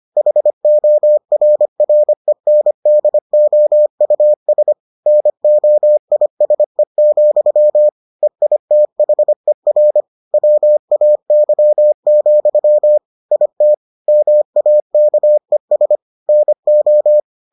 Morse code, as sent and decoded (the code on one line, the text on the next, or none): .... --- .-. .-. -. -.. --- ..- ... / -. --- .. ... --..-- / . .. - .... .-. / .-- .- -.-- --..-- / .. - / -- .- -.- . ... / -. ---
HORRENDOUS NOISE, EITHER WAY, IT MAKES NO